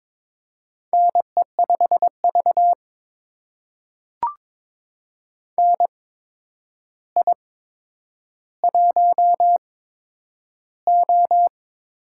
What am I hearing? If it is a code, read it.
Morse code, heard as a keyed tone, about 22 wpm, spaced as for 9 wpm, 700 Hz. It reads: NE5V NI1O
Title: USA Call Signs - 4 Characters - 8x Character Spacing - 22wpm